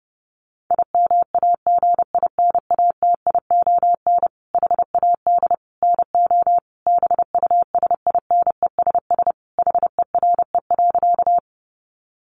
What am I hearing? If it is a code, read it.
Morse code, heard as a keyed tone, about 30 wpm, 700 Hz. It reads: IMAGINATION HAD NO BUSINESS HERE.
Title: Sentences from Top 1300 Words 30wpm